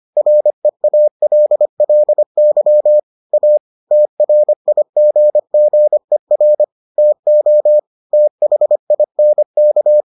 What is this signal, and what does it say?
Morse code, heard as a keyed tone, about 25 wpm, 600 Hz. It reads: REALLY A TRIGGER TO THINK